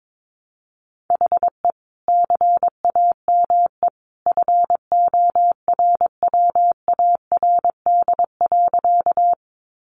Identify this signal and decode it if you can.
Morse code, heard as a keyed tone, about 22 wpm, 700 Hz. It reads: HE CAME FORWARD.
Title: Sentences from Top 1300 Words 22wpm